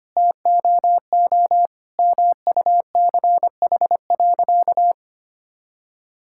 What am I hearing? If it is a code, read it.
Morse code, heard as a keyed tone, about 25 wpm, 700 Hz. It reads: TOO MUCH.
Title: Sentences from Top 1000 Words - Review Part 5 - 25wpm